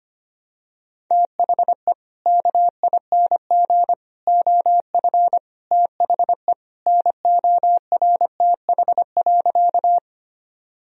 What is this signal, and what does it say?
Morse code, heard as a keyed tone, about 25 wpm, 700 Hz. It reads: THE KING OF THE NORTH.
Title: Sentences from Top 1000 Words - Review Part 1 - 25wpm